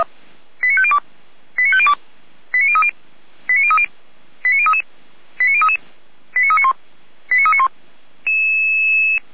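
Five-tone selective call (ZVEI), heard as quick runs of five tones, tones of about 70 ms. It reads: 89502 89502 89030 89030 89033 89033 89391 89391
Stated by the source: ZVEI five-tone selective call received on an SDR.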